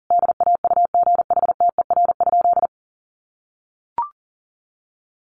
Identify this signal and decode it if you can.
Morse code, heard as a keyed tone, about 40 wpm, 700 Hz. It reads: DAUGHTER?
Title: Sentences from Top 1500 Words 40wpm